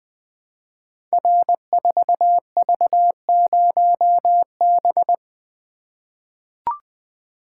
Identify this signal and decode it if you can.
Morse code, heard as a keyed tone, about 20 wpm, 700 Hz. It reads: R4V0B
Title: Code Groups - 5 Letter:Numbers - 20wpm